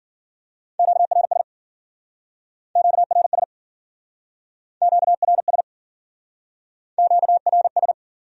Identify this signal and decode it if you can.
Morse code, heard as a keyed tone, about 51 wpm, 700 Hz. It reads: QRS QRS QRS QRS